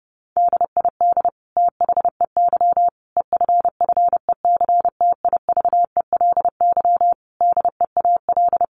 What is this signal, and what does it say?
Morse code, heard as a keyed tone, about 30 wpm, 700 Hz. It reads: DID THEY EFFECTIVELY DEAL
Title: Sentences from Top 1400 Words 30wpm